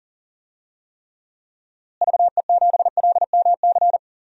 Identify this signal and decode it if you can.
Morse code, heard as a keyed tone, about 40 wpm, 700 Hz. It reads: VE7LMC
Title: DXCC-100 Call Signs - 6 Characters - Repeated 3x - 40wpm